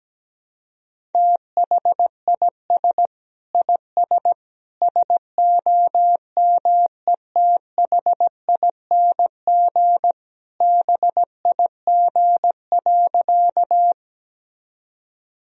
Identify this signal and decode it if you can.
Morse code, heard as a keyed tone, about 17 wpm, 700 Hz. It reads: THIS IS SOMETHING BIG.
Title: Sentences from Top 1000 Words - Review Part 1 - 17wpm